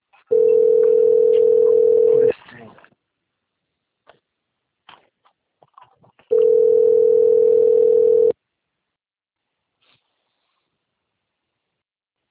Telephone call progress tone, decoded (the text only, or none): ringback tone